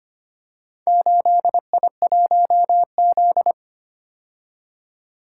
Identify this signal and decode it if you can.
Morse code, heard as a keyed tone, about 25 wpm, 700 Hz. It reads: 8I1Z
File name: practice / 25wpm / Four Letter-Number Combinations - 25wpm